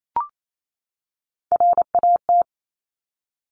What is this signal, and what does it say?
Morse code, heard as a keyed tone, about 28 wpm, 700 Hz. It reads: RAT
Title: Top Three Letter Combinations in Words 28wpm